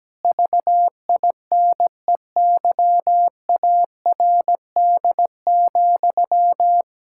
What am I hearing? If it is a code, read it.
Morse code, heard as a keyed tone, about 17 wpm, 700 Hz. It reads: VINEYARD,
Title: Pangram Challenge v2 - 17wpm